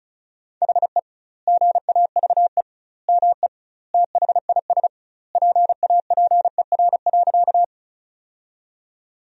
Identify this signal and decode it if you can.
Morse code, heard as a keyed tone, about 36 wpm, 700 Hz. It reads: HE GAVE ME THIS PAPER.
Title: 2x Word-Spacing - Sentences from Top 500 Words - 35wpm